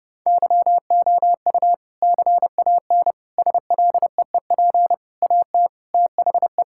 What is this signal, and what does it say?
Morse code, heard as a keyed tone, about 30 wpm, 700 Hz. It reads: YOU CAN SLEEP AT THE
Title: Sentences from Top 1600 Words 30wpm